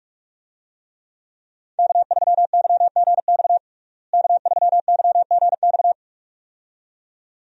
Morse code, heard as a keyed tone, about 42 wpm, 700 Hz. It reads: K3YGX K3YGX
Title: USA Call Signs - 5 Characters - Repeated 2x - 45wpm